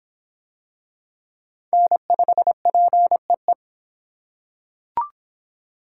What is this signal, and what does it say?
Morse code, heard as a keyed tone, about 23 wpm, 700 Hz. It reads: N5PEE